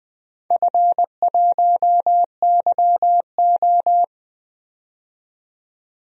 Morse code, heard as a keyed tone, about 20 wpm, 700 Hz. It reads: F1YO
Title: DXCC-100 Call Signs - 4 Characters - 20wpm